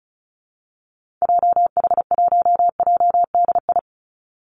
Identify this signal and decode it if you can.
Morse code, heard as a keyed tone, about 34 wpm, 700 Hz. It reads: JH1JDI
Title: Call Signs - 35wpm